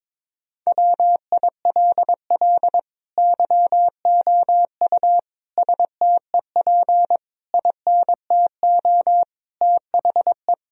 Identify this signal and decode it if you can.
Morse code, heard as a keyed tone, about 21 wpm, 700 Hz. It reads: WILL YOU STEP INTO THE
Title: Sentences from Top 1000 Words - Review Part 2 - 22wpm